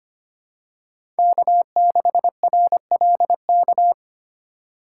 Morse code, heard as a keyed tone, about 25 wpm, 700 Hz. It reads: K6RLK
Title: USA Call Signs - 2x Character Spacing - 25wpm